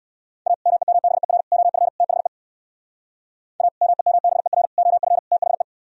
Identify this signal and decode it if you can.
Morse code, heard as a keyed tone, about 64 wpm, 700 Hz. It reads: A MEMBER OF THE A MEMBER OF THE